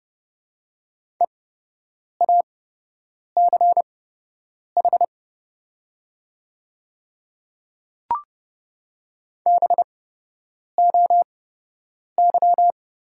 Morse code, heard as a keyed tone, about 30 wpm, spaced as for 8 wpm, 700 Hz. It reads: EACH BOY
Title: 8x Character Spacing - Top 300 Words - 30wpm